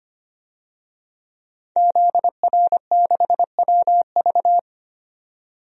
Morse code, heard as a keyed tone, about 25 wpm, 700 Hz. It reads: ZR6WV
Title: DXCC-100 Call Signs - 5 Characters - 25wpm